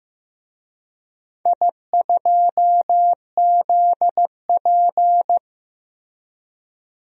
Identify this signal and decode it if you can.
Morse code, heard as a keyed tone, about 15 wpm, 700 Hz. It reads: I2ZP
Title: DXCC-100 Call Signs - 4 Characters - Repeated 3x - 15wpm